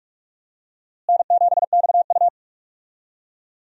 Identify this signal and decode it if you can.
Morse code, heard as a keyed tone, about 46 wpm, 700 Hz. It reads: N7XU
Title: USA Call Signs - 4 Characters - Repeated 3x - 45wpm